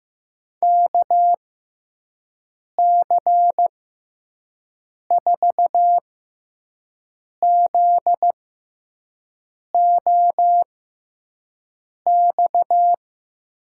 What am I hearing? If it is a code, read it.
Morse code, heard as a keyed tone, about 15 wpm, spaced as for 5 wpm, 700 Hz. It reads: KC4ZOX
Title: USA Call Signs - 6x Character Spacing - 15wpm